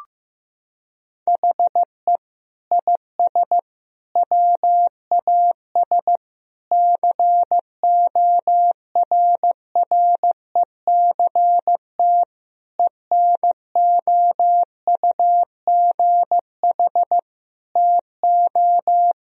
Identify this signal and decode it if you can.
Morse code, heard as a keyed tone, about 15 wpm, 700 Hz. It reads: HE IS WAS CORRECT ENOUGH TO
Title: Sentences from Top 1000 Words - Review Part 2 - 15wpm